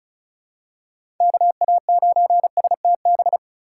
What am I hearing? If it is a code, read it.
Morse code, heard as a keyed tone, about 34 wpm, 700 Hz. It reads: KA9STB
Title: USA Call Signs - 3x Character Spacing - 35wpm